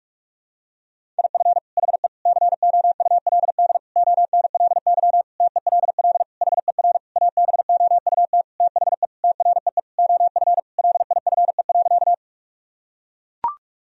Morse code, heard as a keyed tone, about 42 wpm, 700 Hz. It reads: IF HE COULD ONLY TELL HER ABOUT THE TREE OF LIFE.
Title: Sentences from Top 1000 Words - Review Part 1 - 45wpm